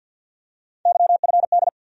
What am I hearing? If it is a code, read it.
Morse code, heard as a keyed tone, about 50 wpm, 700 Hz. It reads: YFD
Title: USA Call Sign Postfixes - 3 Letters 50wpm